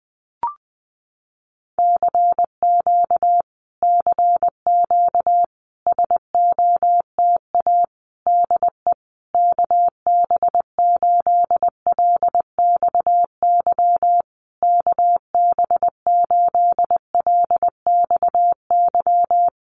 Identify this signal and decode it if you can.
Morse code, heard as a keyed tone, about 20 wpm, 700 Hz. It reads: CQ CQ SOTA DE KB8LXY KB8LXY
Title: Common QSO Phrases Encore 20wpm